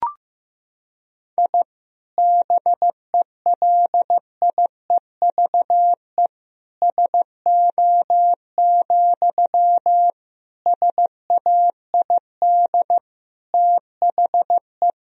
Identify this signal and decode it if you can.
Morse code, heard as a keyed tone, about 16 wpm, 700 Hz. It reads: I BELIEVE SO, SAID THE
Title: Sentences from Top 1000 Words - Review Part 4 - 15wpm